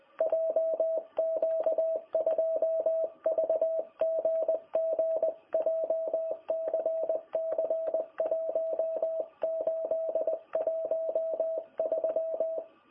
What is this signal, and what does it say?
Morse code, heard as a keyed tone, about 19 wpm, 660 Hz. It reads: JQ2VGG JCC 1813
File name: jh8cbh / 2021cw